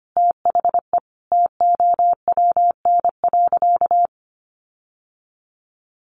Morse code, heard as a keyed tone, about 25 wpm, 700 Hz. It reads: THE TOWN.
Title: Sentences from Top 1400 Words 25wpm